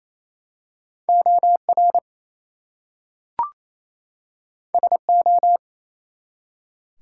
Morse code, heard as a keyed tone, about 28 wpm, 700 Hz. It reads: OR SO
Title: Two Letter Words - 28wpm